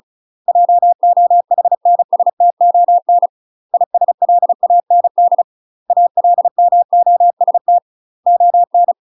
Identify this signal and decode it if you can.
Morse code, heard as a keyed tone, about 34 wpm, 700 Hz. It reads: JOHNSTON ISLAND ALMOST ON
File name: UK MASTER QSOs TRAINING/35WPM